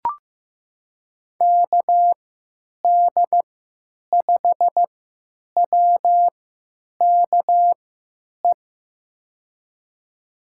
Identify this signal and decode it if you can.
Morse code, heard as a keyed tone, about 15 wpm, spaced as for 9 wpm, 700 Hz. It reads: KD5WKE
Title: USA Call Signs - 3x Character Spacing - 15wpm